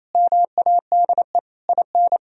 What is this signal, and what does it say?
Morse code, heard as a keyed tone, about 27 wpm, 700 Hz. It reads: MADE IN
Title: Sentences from Top 1000 Words - Review Part 1 - 28wpm